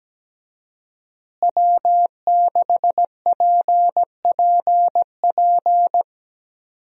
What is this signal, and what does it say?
Morse code, heard as a keyed tone, about 17 wpm, 700 Hz. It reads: W6PPP
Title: USA Call Signs - 5 Characters - Repeated 2x - 17wpm